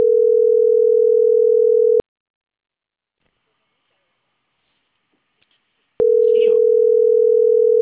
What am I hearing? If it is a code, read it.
Ringback tone (North America), 440 + 480 Hz.